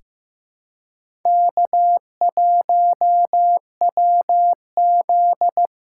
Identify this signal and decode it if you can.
Morse code, heard as a keyed tone, about 16 wpm, 700 Hz. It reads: K1WZ
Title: USA Callsign DE Callsign - 4 Characters - 15wpm